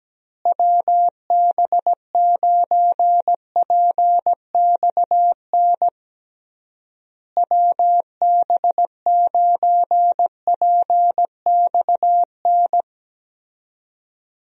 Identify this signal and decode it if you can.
Morse code, heard as a keyed tone, about 17 wpm, 700 Hz. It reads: WB9PXN WB9PXN
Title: USA Call Signs - 6 Characters - Repeated 3x - 17wpm